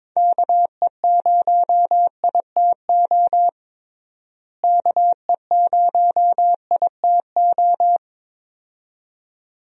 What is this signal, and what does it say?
Morse code, heard as a keyed tone, about 21 wpm, 700 Hz. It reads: KE0ITO KE0ITO
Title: USA Call Signs - 6 Characters - Repeated 2x - 22wpm